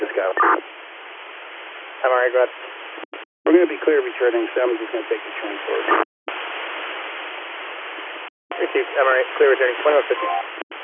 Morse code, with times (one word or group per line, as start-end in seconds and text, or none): none